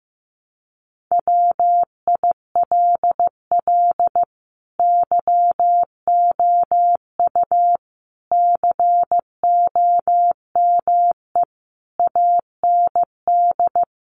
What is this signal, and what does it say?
Morse code, heard as a keyed tone, about 15 wpm, 700 Hz. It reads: WILL YOU COME AND
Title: Sentences from Top 100 Words 15wpm